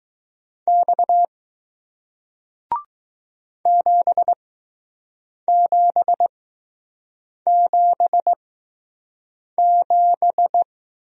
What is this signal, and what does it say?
Morse code, heard as a keyed tone, about 20 wpm, spaced as for 6 wpm, 700 Hz. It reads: X 7777